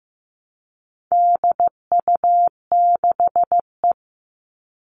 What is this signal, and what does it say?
Morse code, heard as a keyed tone, about 15 wpm, 700 Hz. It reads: DU6E
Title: All Cut Numbers - 4-Digit 15wpm